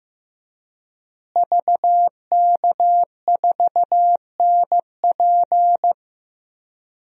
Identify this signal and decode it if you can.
Morse code, heard as a keyed tone, about 16 wpm, 700 Hz. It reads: VK4NP
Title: DXCC-100 Call Signs - 5 Characters - Repeated 3x - 15wpm